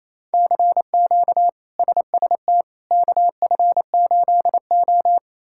Morse code, heard as a keyed tone, about 28 wpm, 700 Hz. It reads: CQ SST KF8O
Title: K1USN SST QSO Practice - 28wpm